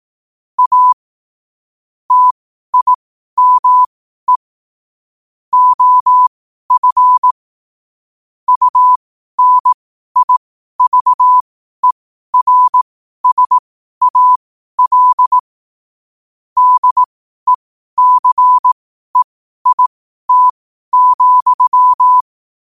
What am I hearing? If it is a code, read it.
Morse code, heard as a keyed tone, about 18 wpm, spaced as for 12 wpm, 1000 Hz. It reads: A TIME OF UNIVERSAL DECEIT,
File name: qotdfiles / 12wpm